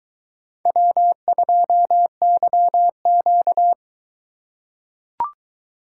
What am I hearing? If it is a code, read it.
Morse code, heard as a keyed tone, about 22 wpm, 700 Hz. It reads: W2YQ